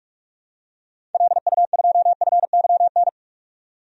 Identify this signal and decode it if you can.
Morse code, heard as a keyed tone, about 46 wpm, 700 Hz. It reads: LU2FYN
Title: DXCC-100 Call Signs - 6 Characters - 45wpm